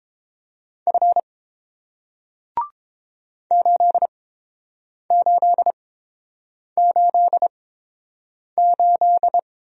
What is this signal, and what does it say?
Morse code, heard as a keyed tone, about 30 wpm, spaced as for 7 wpm, 700 Hz. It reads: F 8888